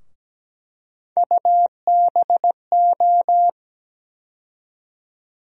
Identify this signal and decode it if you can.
Morse code, heard as a keyed tone, about 17 wpm, 700 Hz. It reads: UBO
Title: USA Call Sign Postfixes - 3 Letters 17wpm